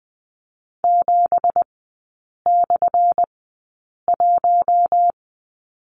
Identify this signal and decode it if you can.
Morse code, heard as a keyed tone, about 20 wpm, spaced as for 8 wpm, 700 Hz. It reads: 7/1
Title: 178 Lesson 29e - Slash - Three Characters 20wpm